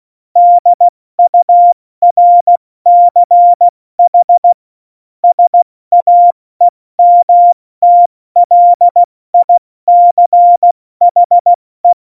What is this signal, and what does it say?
Morse code, heard as a keyed tone, about 16 wpm, spaced as for 14 wpm, 700 Hz. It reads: DURCH SAEMTLICHE
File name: Morsekurs / Witze / Witz 07 April 2020, 70 F80